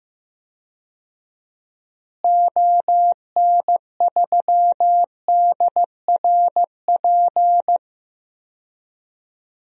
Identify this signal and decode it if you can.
Morse code, heard as a keyed tone, about 15 wpm, 700 Hz. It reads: ON3DRP
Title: DXCC-100 Call Signs - 6 Characters - Repeated 2x - 15wpm